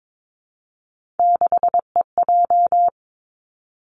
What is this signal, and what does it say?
Morse code, heard as a keyed tone, about 22 wpm, 700 Hz. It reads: 6EJ
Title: Three Challenging Letter-Number Combinations 22wpm